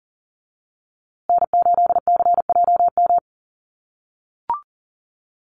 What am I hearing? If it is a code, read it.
Morse code, heard as a keyed tone, about 40 wpm, 700 Hz. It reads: N8CJM
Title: USA Call Signs Repeated 3x 40wpm